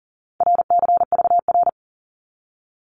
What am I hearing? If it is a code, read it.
Morse code, heard as a keyed tone, about 42 wpm, 700 Hz. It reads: RCVR
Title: Abbreviations 40wpm